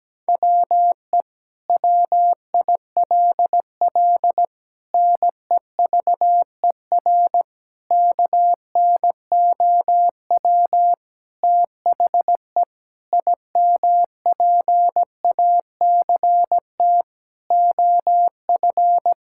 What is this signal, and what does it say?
Morse code, heard as a keyed tone, about 17 wpm, 700 Hz. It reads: WE WILL NEVER KNOW THE IMPACT OF
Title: Sentences from Top 1000 Words - Review Part 6 - 17wpm